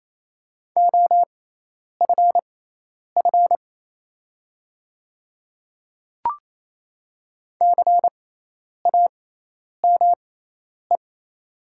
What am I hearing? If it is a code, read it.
Morse code, heard as a keyed tone, about 28 wpm, spaced as for 10 wpm, 700 Hz. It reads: OFF CAME
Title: 6x Character Spacing - Top 300 Words - 28wpm